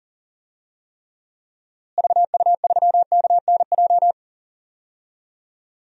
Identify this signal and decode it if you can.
Morse code, heard as a keyed tone, about 40 wpm, 700 Hz. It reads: VU3KNJ
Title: DXCC-100 Call Signs - 6 Characters - Repeated 2x - 40wpm